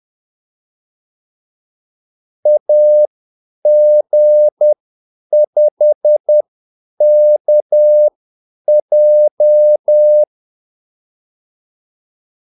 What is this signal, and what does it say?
Morse code, heard as a keyed tone, about 10 wpm, spaced as for 8 wpm, 600 Hz. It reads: AG5KJ
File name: Callsigns-10-8wpm